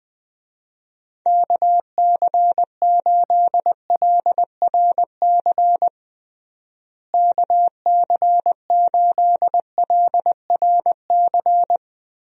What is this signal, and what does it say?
Morse code, heard as a keyed tone, about 20 wpm, 700 Hz. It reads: KC8LRC KC8LRC
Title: USA Call Signs - 6 Characters - Repeated 3x - 20wpm